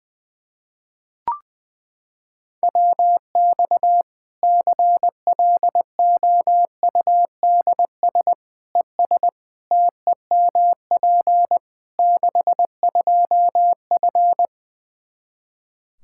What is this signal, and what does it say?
Morse code, heard as a keyed tone, about 20 wpm, 700 Hz. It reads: WX CLOUDS ES TEMP 62F
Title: K1USN SST QSO Practice 20wpm